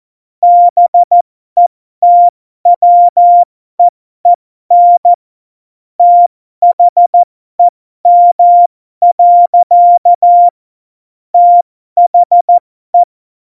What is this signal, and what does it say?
Morse code, heard as a keyed tone, about 14 wpm, spaced as for 12 wpm, 700 Hz. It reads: BETWEEN THEM. THE
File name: Morsekurs / Witze